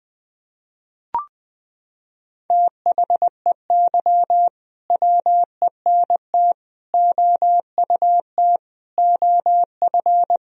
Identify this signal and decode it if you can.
Morse code, heard as a keyed tone, about 19 wpm, 700 Hz. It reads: THEY WENT OUT OF
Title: Sentences from Top 1000 Words - Review Part 4 - 20wpm